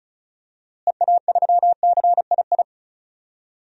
Morse code, heard as a keyed tone, about 36 wpm, 700 Hz. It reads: EA3CII
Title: DXCC-100 Call Signs - 6 Characters - Repeated 2x - 35wpm